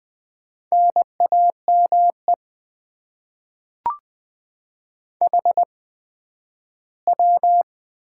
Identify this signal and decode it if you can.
Morse code, heard as a keyed tone, about 20 wpm, spaced as for 12 wpm, 700 Hz. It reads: NAME HW
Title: Top 20 QSO Elements - 8x Character Spacing - 20wpm